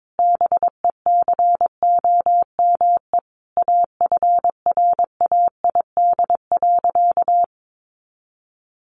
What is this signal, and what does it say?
Morse code, heard as a keyed tone, about 22 wpm, 700 Hz. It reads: BECOME AFRAID.
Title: Sentences from Top 1500 Words 22wpm